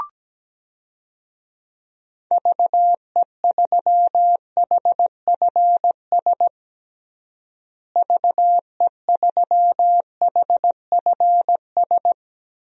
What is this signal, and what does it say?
Morse code, heard as a keyed tone, about 17 wpm, 700 Hz. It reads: VE3HFS VE3HFS
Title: DXCC-100 Call Signs - 6 Characters - Repeated 3x - 17wpm